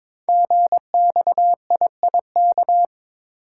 Morse code, heard as a keyed tone, about 23 wpm, 700 Hz. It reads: GXIIK
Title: Code Groups - 5 Letters - 22wpm